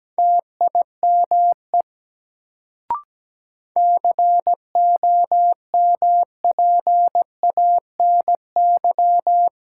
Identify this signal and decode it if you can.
Morse code, heard as a keyed tone, about 17 wpm, 700 Hz. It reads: TIME COMPANY
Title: Sets of 2 Words - New Word Drill - 17wpm